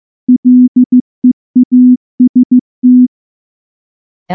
Morse code, heard as a keyed tone, about 14 wpm, 250 Hz. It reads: LEAST
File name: morse-training-15WPM